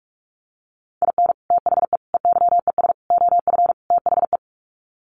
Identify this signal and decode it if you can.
Morse code, heard as a keyed tone, about 44 wpm, 700 Hz. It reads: IN THE EYES OF THE
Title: Sets of 5 Words 45wpm